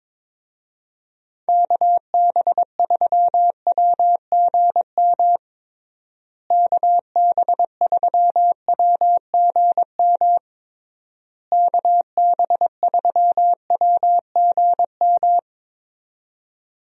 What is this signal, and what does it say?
Morse code, heard as a keyed tone, about 22 wpm, 700 Hz. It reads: KB3WGM KB3WGM KB3WGM
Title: USA Call Signs - 6 Characters - Repeated 3x - 22wpm